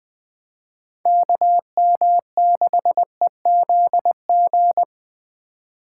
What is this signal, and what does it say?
Morse code, heard as a keyed tone, about 20 wpm, 700 Hz. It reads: KM6EZG
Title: Speed Racing - USA 2x3 Call Signs 17wpm